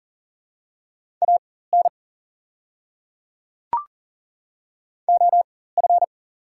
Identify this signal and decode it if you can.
Morse code, heard as a keyed tone, about 40 wpm, spaced as for 19 wpm, 700 Hz. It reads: AN OF
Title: 4x Character Spacing - Top 100 Words - 40wpm